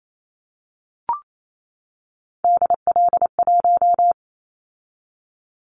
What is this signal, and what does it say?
Morse code, heard as a keyed tone, about 28 wpm, 700 Hz. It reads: DL1
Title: Callsign Prefixes for Top 25 DXCC - 28wpm